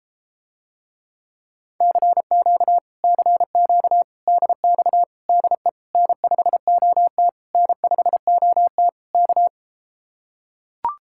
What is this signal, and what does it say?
Morse code, heard as a keyed tone, about 33 wpm, 700 Hz. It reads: CQ CQ DX DE N5OT N5OT K